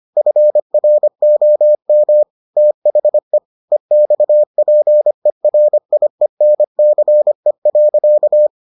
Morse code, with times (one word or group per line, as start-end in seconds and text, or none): 0.2-2.2 FROM
2.6-3.4 THE
3.7-8.5 EXPERIENCE.